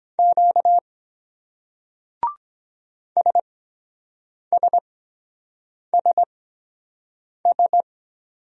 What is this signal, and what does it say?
Morse code, heard as a keyed tone, about 22 wpm, spaced as for 7 wpm, 700 Hz. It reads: Q SSSS